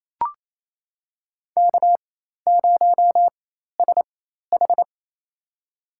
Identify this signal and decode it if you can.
Morse code, heard as a keyed tone, about 28 wpm, spaced as for 13 wpm, 700 Hz. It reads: K0SH
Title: USA Call Signs - 4 Characters - 4x Character Spacing - 28wpm